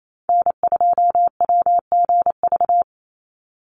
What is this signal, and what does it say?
Morse code, heard as a keyed tone, about 28 wpm, 700 Hz. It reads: N2WGV
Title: USA Call Signs 28wpm